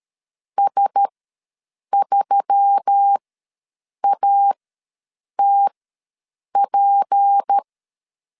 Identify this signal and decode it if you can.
Morse code, heard as a keyed tone, about 13 wpm, spaced as for 7 wpm, 800 Hz. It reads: S3ATP
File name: LEZIONI / 6 PKX0 / 6 P ws 12wpm 3117